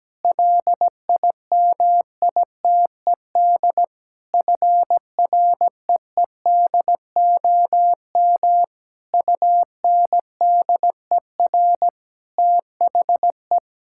Morse code, heard as a keyed tone, about 17 wpm, 700 Hz. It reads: LIMITED FREEDOM UNDER THE
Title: Sentences from Top 1000 Words - Review Part 6 - 17wpm